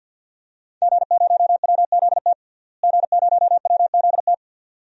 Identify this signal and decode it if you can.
Morse code, heard as a keyed tone, about 51 wpm, 700 Hz. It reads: G0WZT G0WZT
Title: DXCC-100 Call Signs - 5 Characters - Repeated 2x - 50wpm